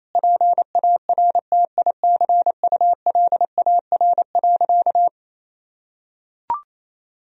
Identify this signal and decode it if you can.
Morse code, heard as a keyed tone, about 28 wpm, 700 Hz. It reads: PARTICULAR.
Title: Sentences from Top 1000 Words - Review Part 4 - 28wpm